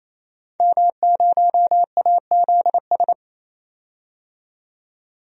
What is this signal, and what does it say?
Morse code, heard as a keyed tone, about 27 wpm, 700 Hz. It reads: M0AZS
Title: DXCC-100 Call Signs - 5 Characters - 28wpm